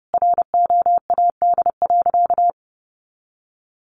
Morse code, heard as a keyed tone, about 30 wpm, 700 Hz. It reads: ROAD.
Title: Sentences from Top 400 Words 30wpm